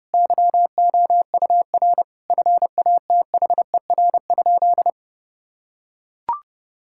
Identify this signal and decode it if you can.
Morse code, heard as a keyed tone, about 30 wpm, 700 Hz. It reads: YOUR FATHER?
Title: Sentences from Top 1600 Words 30wpm